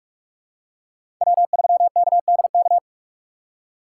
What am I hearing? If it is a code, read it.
Morse code, heard as a keyed tone, about 46 wpm, 700 Hz. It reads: W3KDK
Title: USA Call Signs - 5 Characters - 4x Character Spacing - 45wpm